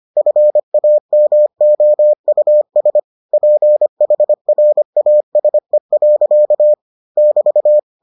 Morse code, heard as a keyed tone, about 25 wpm, 600 Hz. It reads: FAMOUS PHRASE. <BT>